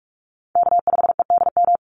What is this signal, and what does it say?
Morse code, heard as a keyed tone, about 44 wpm, 700 Hz. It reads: K5EDM
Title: Call Signs - 45wpm